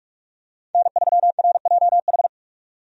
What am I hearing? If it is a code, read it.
Morse code, heard as a keyed tone, about 46 wpm, 700 Hz. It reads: N3RJH
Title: USA Call Signs - 5 Characters - 45wpm